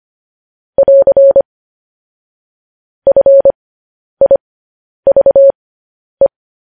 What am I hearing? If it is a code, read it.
Morse code, heard as a keyed tone, about 25 wpm, spaced as for 10 wpm, 550 Hz. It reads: <AR> FIVE
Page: news headlines in morse code at 10 wpm